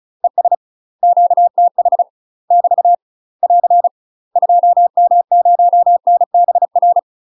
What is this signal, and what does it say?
Morse code, heard as a keyed tone, about 34 wpm, 700 Hz. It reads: ES QTH <BT> <AR> 2M0NBR
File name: UK MASTER QSOs TRAINING/35WPM